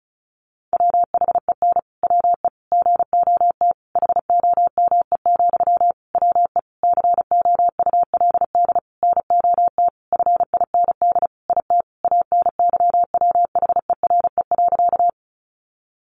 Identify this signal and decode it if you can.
Morse code, heard as a keyed tone, about 34 wpm, 700 Hz. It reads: WHEN WE GOT HOME, WE COULD NOT FIND IT ANYWHERE.
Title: Sentences from Top 1400 Words 35wpm